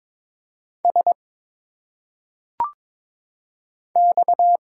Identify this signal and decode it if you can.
Morse code, heard as a keyed tone, about 22 wpm, 700 Hz. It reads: S X
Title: Single Letters - Phonetic Pronunciation 22wpm